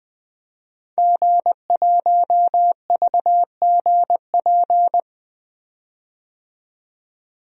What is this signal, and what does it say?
Morse code, heard as a keyed tone, about 21 wpm, 700 Hz. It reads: G1VGP
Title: DXCC-100 Call Signs - 5 Characters - 20wpm